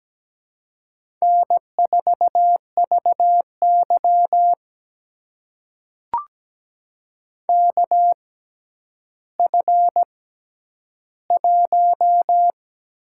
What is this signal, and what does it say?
Morse code, heard as a keyed tone, about 17 wpm, spaced as for 10 wpm, 700 Hz. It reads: N4VY KF1
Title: USA Call Signs - 4 Characters - 6x Character Spacing - 17wpm